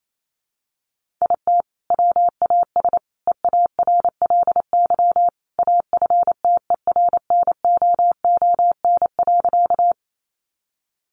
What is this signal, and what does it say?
Morse code, heard as a keyed tone, about 28 wpm, 700 Hz. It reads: IT WAS EARLY AFTERNOON.
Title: Sentences from Top 1500 Words 28wpm